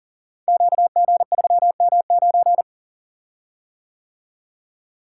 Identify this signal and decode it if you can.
Morse code, heard as a keyed tone, about 38 wpm, 690 Hz. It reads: QG3M9
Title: Code Groups - 5 Letter:Numbers - 40wpm